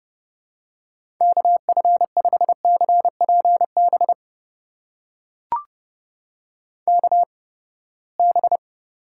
Morse code, heard as a keyed tone, about 30 wpm, spaced as for 21 wpm, 700 Hz. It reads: KF5CPB KB
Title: USA Call Signs - 6 Characters - 8x Character Spacing - 30wpm